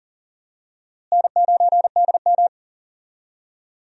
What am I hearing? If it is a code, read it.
Morse code, heard as a keyed tone, about 40 wpm, 700 Hz. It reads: N9DM
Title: USA Call Signs - 4 Characters - Repeated 2x - 40wpm